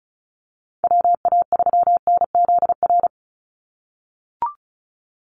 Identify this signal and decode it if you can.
Morse code, heard as a keyed tone, about 34 wpm, 700 Hz. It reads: WA3NZR
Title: Call Signs - 35wpm